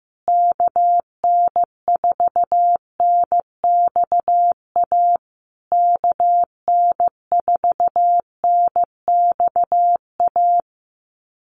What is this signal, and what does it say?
Morse code, heard as a keyed tone, about 15 wpm, 700 Hz. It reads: KN4NXA KN4NXA
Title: USA Call Signs Repeated 3x 15wpm